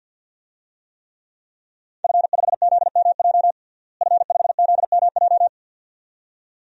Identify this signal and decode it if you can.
Morse code, heard as a keyed tone, about 53 wpm, 700 Hz. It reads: F5ZMJ F5ZMJ
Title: DXCC-100 Call Signs - 5 Characters - Repeated 2x - 50wpm